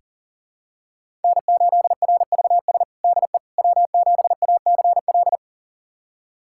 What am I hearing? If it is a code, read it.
Morse code, heard as a keyed tone, about 40 wpm, 700 Hz. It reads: N8RVS DE W7ACL
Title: USA Callsign DE Callsign - 5 Characters - 40wpm